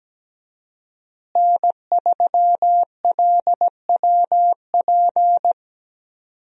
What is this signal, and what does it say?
Morse code, heard as a keyed tone, about 17 wpm, 700 Hz. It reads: N3LWP
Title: USA Call Signs - 5 Characters - Repeated 2x - 17wpm